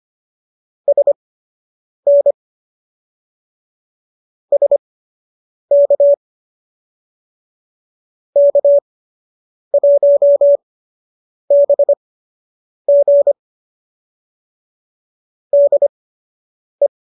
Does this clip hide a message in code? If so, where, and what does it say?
Yes: Morse code from 0.9 to 16.9 s, heard as a keyed tone, about 25 wpm, spaced as for 8 wpm, 570 Hz. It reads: SN SK K1BG DE